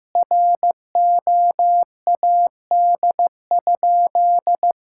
Morse code, heard as a keyed tone, about 15 wpm, 700 Hz. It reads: ROAD?
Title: Sentences from Top 1000 Words - Review Part 5 - 15wpm